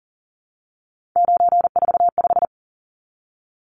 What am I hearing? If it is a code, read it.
Morse code, heard as a keyed tone, about 40 wpm, 700 Hz. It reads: 945
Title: Numbers - 3-Digit 40wpm